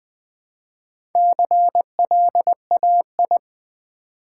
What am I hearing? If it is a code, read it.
Morse code, heard as a keyed tone, about 20 wpm, 700 Hz. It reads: CLAI
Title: Build A Word - 1 Letter At A Time - 5-Letter Words - 20wpm